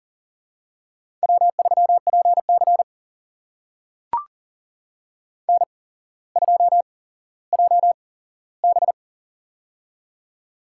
Morse code, heard as a keyed tone, about 39 wpm, spaced as for 17 wpm, 700 Hz. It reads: W3PC N2JB
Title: USA Call Signs - 4 Characters - 8x Character Spacing - 40wpm